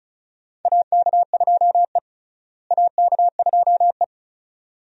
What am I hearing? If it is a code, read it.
Morse code, heard as a keyed tone, about 36 wpm, 700 Hz. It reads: AK2E AK2E
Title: USA Call Signs - 4 Characters - Repeated 2x - 35wpm